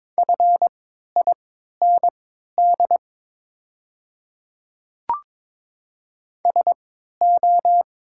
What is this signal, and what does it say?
Morse code, heard as a keyed tone, about 22 wpm, spaced as for 13 wpm, 700 Hz. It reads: FIND SO
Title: 3x Character Spacing - Top 200 Words - 22wpm